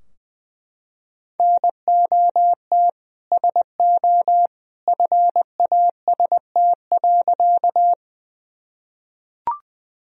Morse code, heard as a keyed tone, about 20 wpm, 700 Hz. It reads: NOT SO FAST.
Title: Sentences from Top 1000 Words - Review Part 2 - 20wpm